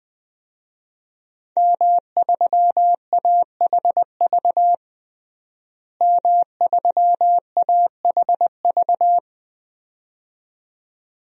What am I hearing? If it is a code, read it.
Morse code, heard as a keyed tone, about 21 wpm, 700 Hz. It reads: M3AHV M3AHV
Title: DXCC-100 Call Signs - 5 Characters - Repeated 2x - 20wpm